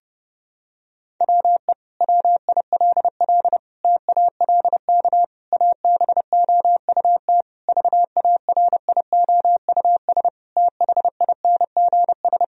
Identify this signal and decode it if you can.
Morse code, heard as a keyed tone, about 30 wpm, 700 Hz. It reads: WE WILL TALK ABOUT VARIOUS THINGS
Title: Sentences from Top 1000 Words - Review Part 4 - 30wpm